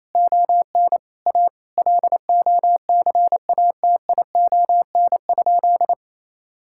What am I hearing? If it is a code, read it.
Morse code, heard as a keyed tone, about 28 wpm, 700 Hz. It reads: ON A LOCATION?
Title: Sentences from Top 1000 Words - Review Part 6 - 28wpm